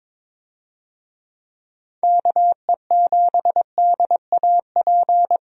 Morse code, heard as a keyed tone, about 22 wpm, 700 Hz. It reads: KE7DAP
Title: USA Call Signs - 6 Characters - 22wpm